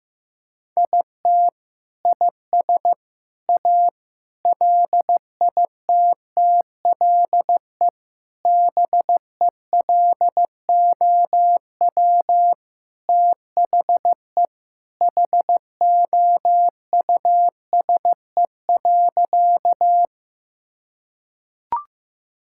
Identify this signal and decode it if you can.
Morse code, heard as a keyed tone, about 15 wpm, 700 Hz. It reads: IT IS A LITTLE BELOW THE HOUSE.
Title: Sentences from Top 1000 Words - Review Part 5 - 15wpm